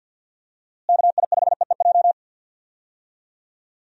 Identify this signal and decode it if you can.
Morse code, heard as a keyed tone, about 50 wpm, 700 Hz. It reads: KI5EEJ
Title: USA Call Signs - 6 Characters - Repeated 3x - 50wpm